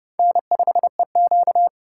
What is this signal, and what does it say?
Morse code, heard as a keyed tone, about 38 wpm, 700 Hz. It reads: N5EQ
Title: Speed Racing - USA 1x2 Call Signs 30wpm